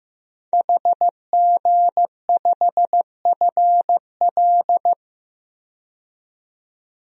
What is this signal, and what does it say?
Morse code, heard as a keyed tone, about 16 wpm, 700 Hz. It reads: HG5FL
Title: DXCC-100 Call Signs - 5 Characters - 15wpm